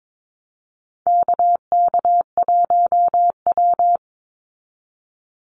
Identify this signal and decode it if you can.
Morse code, heard as a keyed tone, about 22 wpm, 700 Hz. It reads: KK1W
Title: NA SOTA Activator Callsigns 22wpm